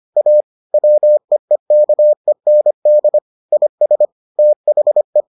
Morse code, heard as a keyed tone, about 25 wpm, 600 Hz. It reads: A WEEKEND IS THE